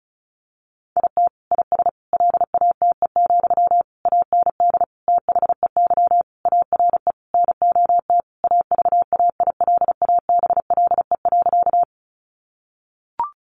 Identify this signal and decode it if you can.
Morse code, heard as a keyed tone, about 34 wpm, 700 Hz. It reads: IT IS LATE, AND THEY ARE NOT AVAILABLE.
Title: Sentences from Top 600 Words 35wpm